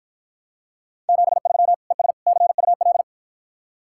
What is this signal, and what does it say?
Morse code, heard as a keyed tone, about 52 wpm, 700 Hz. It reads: 73 ES CUL